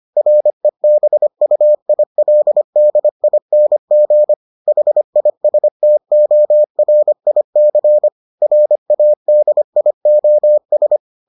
Morse code, heard as a keyed tone, about 25 wpm, 600 Hz. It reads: REBUILDING HISTORIC RADIOS